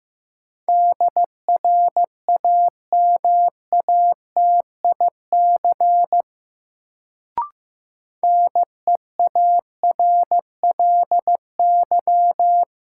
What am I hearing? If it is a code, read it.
Morse code, heard as a keyed tone, about 16 wpm, 700 Hz. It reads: DRAMATIC NEARLY
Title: Sentences from Top 1300 Words - New Word Drill - 15wpm